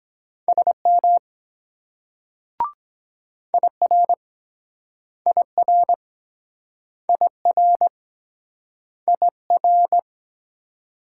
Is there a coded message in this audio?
Yes: Morse code from 0.5 to 10.0 s, heard as a keyed tone, about 23 wpm, 700 Hz. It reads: SM IR IR IR IR